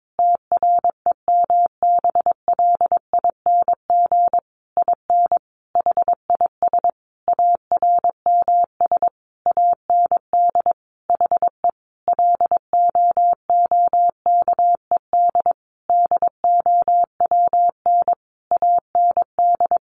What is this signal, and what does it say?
Morse code, heard as a keyed tone, about 22 wpm, 700 Hz. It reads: TREMBLING IN HIS ARMS AND HE LOOKED DOWN AND
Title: A Sound Of Thunder 22wpm